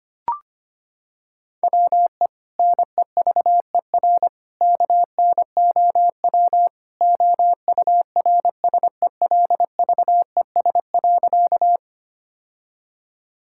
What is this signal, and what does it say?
Morse code, heard as a keyed tone, about 25 wpm, 700 Hz. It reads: WE NEVER KNOW OURSELVES.
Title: Sentences from Top 1600 Words 25wpm